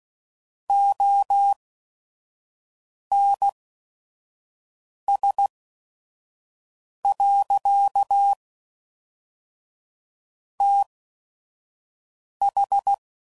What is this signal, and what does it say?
Morse code, heard as a keyed tone, about 16 wpm, spaced as for 5 wpm, 800 Hz. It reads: ONS. TH
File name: CW Code Groups @5WPM / Technical Subjects / SwPwrSup